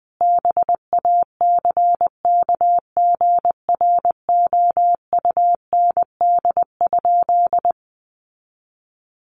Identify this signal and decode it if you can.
Morse code, heard as a keyed tone, about 19 wpm, 700 Hz. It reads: BACKGROUND?
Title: Sentences from Top 1300 Words 20wpm